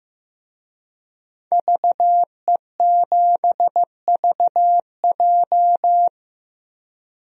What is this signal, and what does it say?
Morse code, heard as a keyed tone, about 16 wpm, 700 Hz. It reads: VE7VJ
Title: DXCC-100 Call Signs - 5 Characters - 15wpm